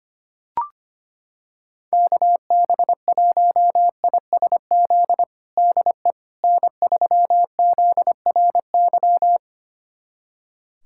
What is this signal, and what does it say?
Morse code, heard as a keyed tone, about 25 wpm, 700 Hz. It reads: KB1ISZ DE N3ZRY
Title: USA Callsign DE Callsign 25wpm